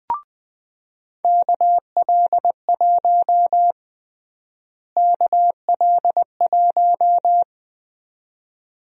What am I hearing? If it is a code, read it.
Morse code, heard as a keyed tone, about 20 wpm, 700 Hz. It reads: KL1 KL1
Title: USA Call Sign Prefixes - Repeated 2x - 20wpm